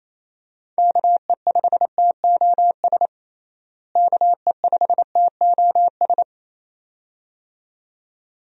Morse code, heard as a keyed tone, about 28 wpm, 700 Hz. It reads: KE5TOS KE5TOS
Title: USA Call Signs - 6 Characters - Repeated 2x - 28wpm